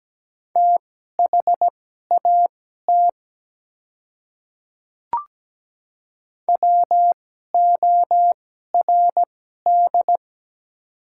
Morse code, heard as a keyed tone, about 17 wpm, spaced as for 12 wpm, 700 Hz. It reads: THAT WORD
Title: 2x Character Spacing - Top 300 Words - 17wpm